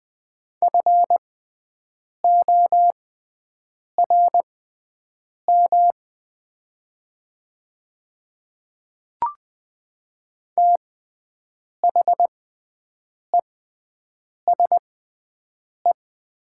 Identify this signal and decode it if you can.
Morse code, heard as a keyed tone, about 20 wpm, spaced as for 7 wpm, 700 Hz. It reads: FORM THESE
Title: 6x Character Spacing - Top 200 Words - 20wpm